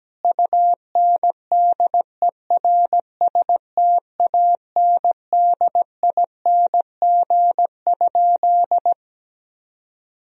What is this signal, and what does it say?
Morse code, heard as a keyed tone, about 17 wpm, 700 Hz. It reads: UNDERSTANDING?
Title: Sentences from Top 1000 Words - Review Part 3 - 17wpm